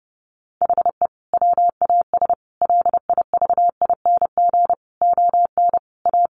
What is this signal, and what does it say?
Morse code, heard as a keyed tone, about 30 wpm, 700 Hz. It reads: HE WAS LIVING ON A
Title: Sentences from Top 1200 Words 30wpm